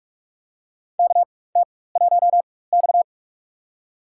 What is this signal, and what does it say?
Morse code, heard as a keyed tone, about 46 wpm, spaced as for 21 wpm, 700 Hz. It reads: KT1X
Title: USA Call Signs - 4x Character Spacing - 45wpm